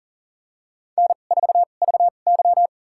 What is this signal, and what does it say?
Morse code, heard as a keyed tone, about 40 wpm, spaced as for 29 wpm, 700 Hz. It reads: N4VY
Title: USA Call Signs - 4 Characters - 2x Character Spacing - 40wpm